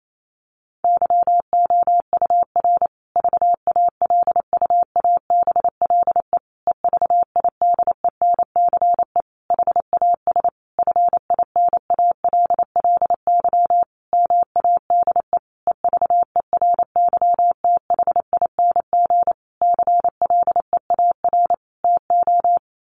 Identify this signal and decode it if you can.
Morse code, heard as a keyed tone, about 28 wpm, 700 Hz. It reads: YOUR VALUABLE EVIDENCE HAS FINALLY MADE EVERYTHING CLEAR TO
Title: Sentences from Top 1300 Words 28wpm